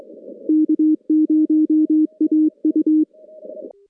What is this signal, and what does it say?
Morse code, heard as a keyed tone, about 22 wpm, 310 Hz. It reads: K0AU
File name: contests / CW-SS-k5tr- -2200